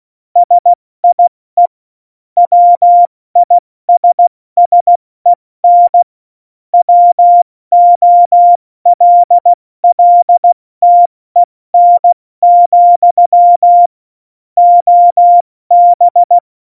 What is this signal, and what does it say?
Morse code, heard as a keyed tone, about 16 wpm, 700 Hz. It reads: SIE WISSEN WOLLTEN, OB
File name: Morsekurs / Witze / Witz 08. Januar 2019, 70 F80